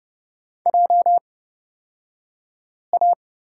Morse code, heard as a keyed tone, about 30 wpm, spaced as for 5 wpm, 700 Hz. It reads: JA
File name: Lesson 38d - Letter q - Two Characters 30wpm